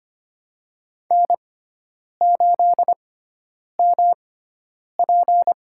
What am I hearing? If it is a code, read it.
Morse code, heard as a keyed tone, about 25 wpm, spaced as for 9 wpm, 700 Hz. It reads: N8MP
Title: USA Call Signs - 4 Characters - 6x Character Spacing - 25wpm